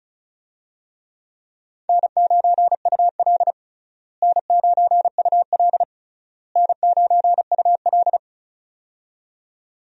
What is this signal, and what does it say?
Morse code, heard as a keyed tone, about 36 wpm, 700 Hz. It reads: N9UL N9UL N9UL
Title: USA Call Signs - 4 Characters - Repeated 3x - 35wpm